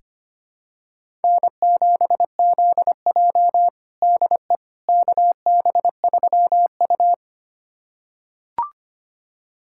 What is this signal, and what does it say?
Morse code, heard as a keyed tone, about 25 wpm, 700 Hz. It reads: N7ZJ DE KB3U
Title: USA Callsign DE Callsign - 4 Characters - 25wpm